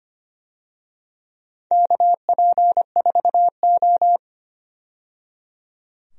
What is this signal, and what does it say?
Morse code, heard as a keyed tone, about 25 wpm, 700 Hz. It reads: KP4O
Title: DXCC-100 Call Signs - 4 Characters - 25wpm